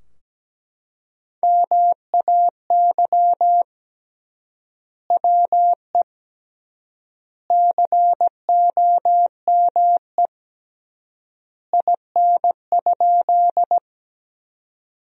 Morse code, heard as a keyed tone, about 17 wpm, 700 Hz. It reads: MAY WE COME IN?
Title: 3x Word-Spacing - Sentences from Top 100 Words - 17wpm